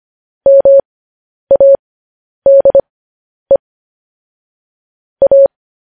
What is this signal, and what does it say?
Morse code, heard as a keyed tone, about 25 wpm, spaced as for 10 wpm, 550 Hz. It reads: MADE A
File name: podcast / News headlines-1x-25x10